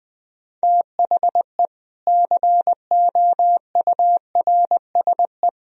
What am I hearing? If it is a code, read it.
Morse code, heard as a keyed tone, about 20 wpm, 700 Hz. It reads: THE COURSE
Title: Speed-Racing - Sets of 4 Words 17wpm